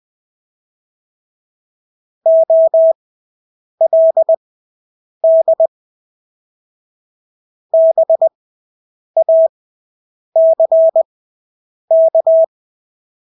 Morse code, heard as a keyed tone, about 20 wpm, spaced as for 8 wpm, 650 Hz. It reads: OLD BACK
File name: Engelske ord 20/8 wpm